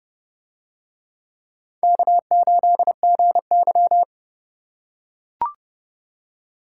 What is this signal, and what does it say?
Morse code, heard as a keyed tone, about 29 wpm, 700 Hz. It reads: K8GY